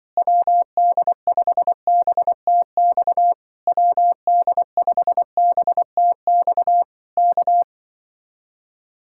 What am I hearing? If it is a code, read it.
Morse code, heard as a keyed tone, about 24 wpm, 700 Hz. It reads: WD5BTX WD5BTX K
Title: Speed-Racing Common QSO Phrases - 20wpm